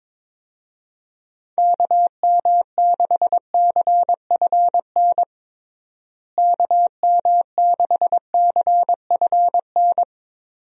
Morse code, heard as a keyed tone, about 22 wpm, 700 Hz. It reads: KM6CFN KM6CFN
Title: USA Call Signs - 6 Characters - Repeated 2x - 22wpm